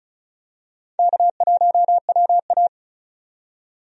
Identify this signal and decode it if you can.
Morse code, heard as a keyed tone, about 34 wpm, 700 Hz. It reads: K1WA
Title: USA Call Signs - 4 Characters - 35wpm